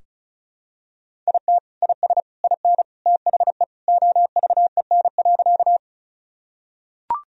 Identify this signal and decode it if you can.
Morse code, heard as a keyed tone, about 34 wpm, 700 Hz. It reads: IT IS IN THE OVEN.
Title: Sentences from Top 1000 Words - Review Part 4 - 35wpm